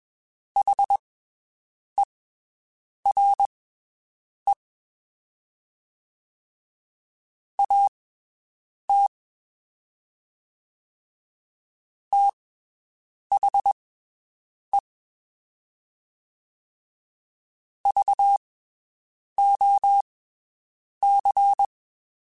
Morse code, heard as a keyed tone, about 21 wpm, spaced as for 7 wpm, 800 Hz. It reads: HERE AT THE VOC
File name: (spaced) / QSOs Spaced-mp3s / WD0BGP 1 @20 sp